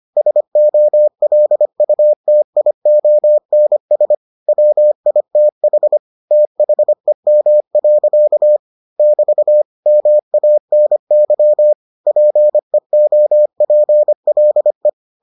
Morse code, heard as a keyed tone, about 25 wpm, 600 Hz. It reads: SOLUTIONS WITH THEM. <BT> MANY PEOPLE